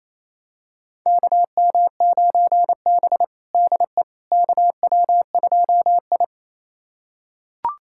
Morse code, heard as a keyed tone, about 28 wpm, 700 Hz. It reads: KM9B DE KW2I
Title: USA Callsign DE Callsign - 4 Characters - 28wpm